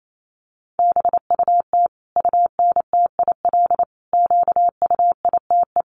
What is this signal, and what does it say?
Morse code, heard as a keyed tone, about 28 wpm, 700 Hz. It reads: BUT UNTIL QUITE
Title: Sentences from Top 1200 Words 28wpm